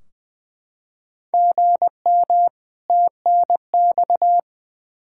Morse code, heard as a keyed tone, about 20 wpm, 700 Hz. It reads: GM TNX
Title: Practice QSO - Exchange 1 - Beginning - 20wpm